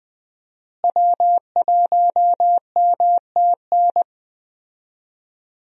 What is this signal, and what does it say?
Morse code, heard as a keyed tone, about 20 wpm, 700 Hz. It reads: W1MTN
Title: USA Call Signs - 5 Characters - Repeated 3x - 20wpm